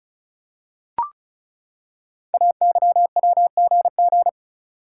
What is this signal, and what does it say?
Morse code, heard as a keyed tone, about 36 wpm, 690 Hz. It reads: AYWGG
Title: Code Groups - 5 Letters - 35wpm